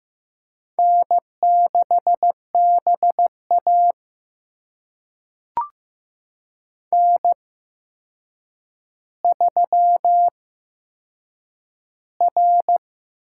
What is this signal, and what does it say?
Morse code, heard as a keyed tone, about 15 wpm, spaced as for 7 wpm, 700 Hz. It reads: N6BA N3R
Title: USA Call Signs - 4 Characters - 8x Character Spacing - 15wpm